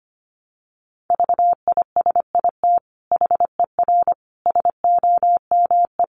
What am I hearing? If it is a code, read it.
Morse code, heard as a keyed tone, about 25 wpm, 700 Hz. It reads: VISIT HER SOME
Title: Sentences from Top 900 Words 25wpm